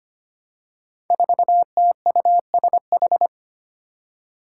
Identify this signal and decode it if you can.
Morse code, heard as a keyed tone, about 25 wpm, 700 Hz. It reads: 4TUSH
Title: Five Challenging Letter-Number Combinations 25wpm